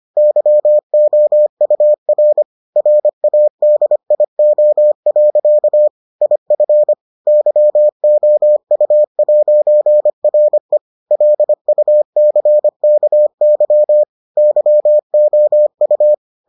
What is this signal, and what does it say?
Morse code, heard as a keyed tone, about 25 wpm, 600 Hz. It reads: YOUR RADIO. IF YOU'RE LUCKY YOU